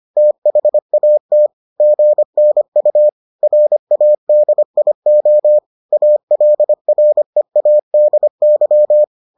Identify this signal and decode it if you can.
Morse code, heard as a keyed tone, about 25 wpm, 600 Hz. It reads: THAT GNU RADIO ALREADY